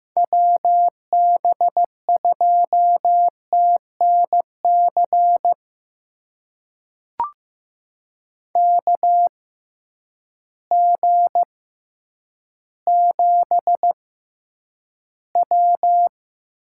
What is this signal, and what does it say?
Morse code, heard as a keyed tone, about 15 wpm, spaced as for 9 wpm, 700 Hz. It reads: WB2TNC KG7W
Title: USA Call Signs - 6 Characters - 6x Character Spacing - 15wpm